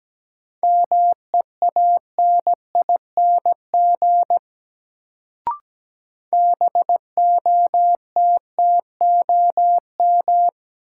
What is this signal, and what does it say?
Morse code, heard as a keyed tone, about 17 wpm, 700 Hz. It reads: MEANING BOTTOM
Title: Sentences from Top 900 Words - New Word Drill - 17wpm